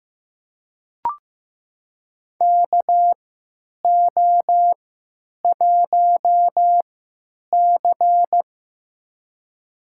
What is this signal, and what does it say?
Morse code, heard as a keyed tone, about 15 wpm, spaced as for 9 wpm, 700 Hz. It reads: KO1C
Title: USA Call Signs - 4 Characters - 3x Character Spacing - 15wpm